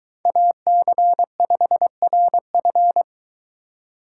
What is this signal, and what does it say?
Morse code, heard as a keyed tone, about 22 wpm, 700 Hz. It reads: AC5RF